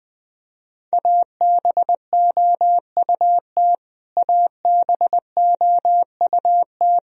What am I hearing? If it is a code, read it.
Morse code, heard as a keyed tone, about 21 wpm, 700 Hz. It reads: ABOUT ABOUT
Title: Top 100 Words Repeated 20wpm